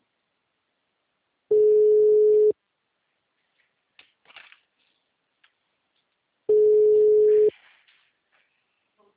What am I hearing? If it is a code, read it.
Ringback tone (Europe, 425 Hz).
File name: out- -2031- - - .3